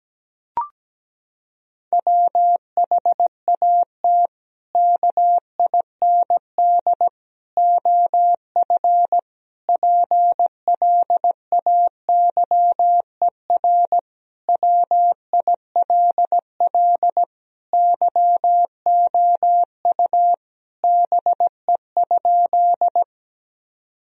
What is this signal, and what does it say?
Morse code, heard as a keyed tone, about 17 wpm, 700 Hz. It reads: WHAT KIND OF PLAYER WILL YOU BE?
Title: Sentences from Top 1000 Words - Review Part 4 - 17wpm